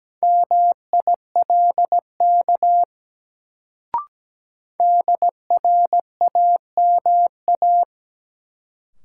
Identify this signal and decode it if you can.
Morse code, heard as a keyed tone, about 17 wpm, 700 Hz. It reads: MILK DRAMA
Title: Sentences from Top 1500 Words - New Word Drill - 17wpm